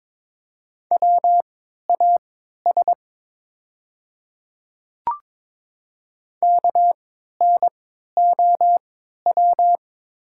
Morse code, heard as a keyed tone, about 22 wpm, spaced as for 12 wpm, 700 Hz. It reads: WAS KNOW
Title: 3x Character Spacing - Top 100 Words - 22wpm